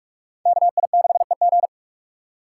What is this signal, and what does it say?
Morse code, heard as a keyed tone, about 46 wpm, 700 Hz. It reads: KI6EG
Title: USA Call Signs - 2x Character Spacing - 45wpm